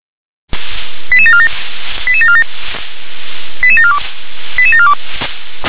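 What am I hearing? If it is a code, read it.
Five-tone selective call (ZVEI), heard as quick runs of five tones, tones of about 70 ms. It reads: 88747 88747 88742 88742
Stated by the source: Received by SDR as a ZVEI five-tone selective call.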